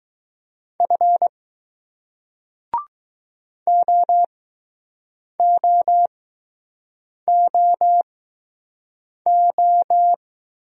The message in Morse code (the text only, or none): F OOOO